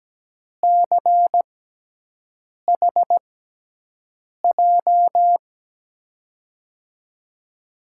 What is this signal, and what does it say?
Morse code, heard as a keyed tone, about 17 wpm, 700 Hz. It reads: C H J
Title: USA Call Signs - 6 Characters - 6x Character Spacing - 17wpm